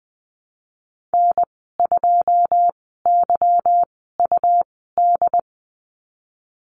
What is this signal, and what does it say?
Morse code, heard as a keyed tone, about 20 wpm, 700 Hz. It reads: N 2 Y U D
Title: 154 Lesson 25g - Letter p - Call Signs 20wpm